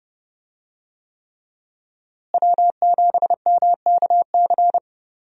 Morse code, heard as a keyed tone, about 30 wpm, 700 Hz. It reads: W7MKC
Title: USA Call Signs - 5 Characters - Repeated 3x - 30wpm